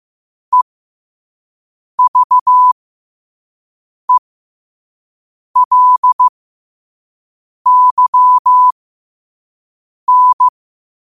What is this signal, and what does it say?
Morse code, heard as a keyed tone, about 15 wpm, spaced as for 5 wpm, 1000 Hz. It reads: EVELYN
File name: qotdfiles / 5wpm